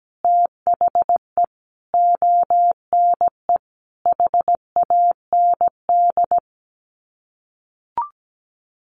Morse code, heard as a keyed tone, about 17 wpm, 700 Hz. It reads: THE ONE HAND
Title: Sets of 4 Words 17wpm